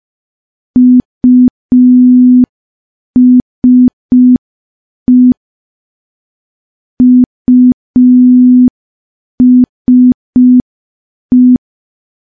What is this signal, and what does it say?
Morse code, heard as a keyed tone, about 5 wpm, 250 Hz. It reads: USE USE
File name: morse / morse-training-05WPM